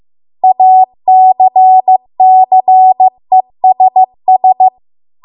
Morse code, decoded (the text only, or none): ACCESS